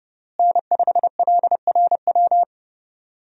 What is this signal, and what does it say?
Morse code, heard as a keyed tone, about 32 wpm, 700 Hz. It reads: N5LRW
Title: USA Call Signs - 5 Characters - Repeated 2x - 30wpm